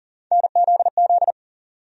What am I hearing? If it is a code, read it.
Morse code, heard as a keyed tone, about 40 wpm, 700 Hz. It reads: NZZ
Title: USA Call Sign Postfixes - 3 Letters 40wpm